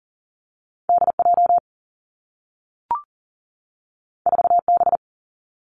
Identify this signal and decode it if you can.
Morse code, heard as a keyed tone, about 42 wpm, 700 Hz. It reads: D J 4 B
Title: Two Challenging Letter-Number Combinations 40wpm